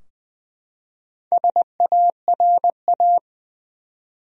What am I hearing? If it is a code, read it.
Morse code, heard as a keyed tone, about 20 wpm, 700 Hz. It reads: SARA
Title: USA Common Names - Max 4 Letters - 20wpm